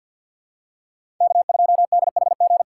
Morse code, heard as a keyed tone, about 53 wpm, 700 Hz. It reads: K2DHG
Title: USA Call Signs - 5 Characters - 8x Character Spacing - 50wpm